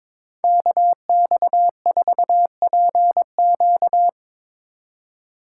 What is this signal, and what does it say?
Morse code, heard as a keyed tone, about 22 wpm, 700 Hz. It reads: KX4PQ
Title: K1USN SST QSO Practice - 22wpm